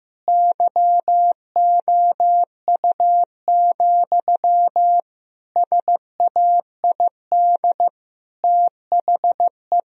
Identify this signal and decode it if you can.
Morse code, heard as a keyed tone, about 16 wpm, 700 Hz. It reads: YOU, SAID THE
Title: Sentences from Top 1000 Words - Review Part 3 - 15wpm